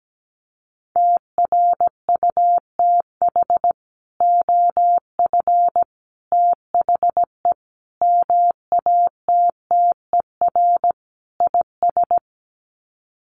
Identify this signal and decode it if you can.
Morse code, heard as a keyed tone, about 17 wpm, 700 Hz. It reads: TRUTH OF THE MATTER IS
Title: Sets of 5 Words Encore2 17wpm